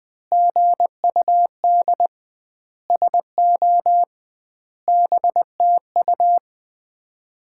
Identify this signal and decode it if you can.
Morse code, heard as a keyed tone, about 20 wpm, 700 Hz. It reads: GUD SO BTU
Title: Common QSO Phrases - 2x Word-Spacing - 20wpm